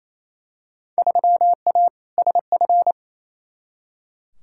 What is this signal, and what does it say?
Morse code, heard as a keyed tone, about 28 wpm, 700 Hz. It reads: 3A SF
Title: Field Day Exchanges Encore - 28wpm